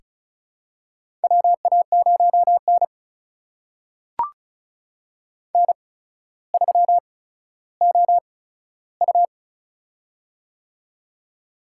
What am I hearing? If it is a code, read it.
Morse code, heard as a keyed tone, about 36 wpm, spaced as for 15 wpm, 700 Hz. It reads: WA0N N3OU
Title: USA Call Signs - 4 Characters - 8x Character Spacing - 35wpm